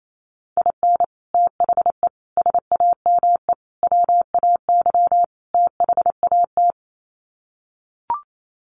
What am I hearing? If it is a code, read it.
Morse code, heard as a keyed tone, about 28 wpm, 700 Hz. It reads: IN THE SAME WAY THAT
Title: Sets of 5 Words 28wpm